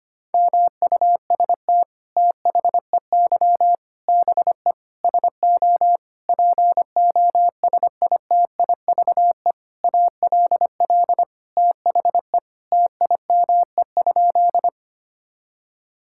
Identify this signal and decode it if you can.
Morse code, heard as a keyed tone, about 25 wpm, 700 Hz. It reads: MUST THEY BE SO POSITIVE ALL THE TIME?
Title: Sentences from Top 1000 Words - Review Part 5 - 25wpm